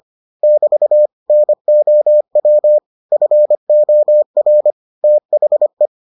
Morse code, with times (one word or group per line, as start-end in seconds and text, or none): 0.4-1.1 <BT>
1.3-2.8 NOW
3.1-4.7 FOR
5.0-5.9 THE